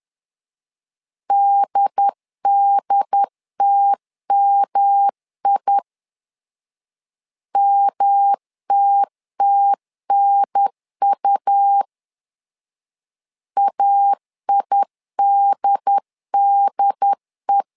Morse code, heard as a keyed tone, about 11 wpm, 800 Hz. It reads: DDTMI MTTNU AIDDE
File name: LEZIONI / 3 MWRO / 3 M 10wpm 1828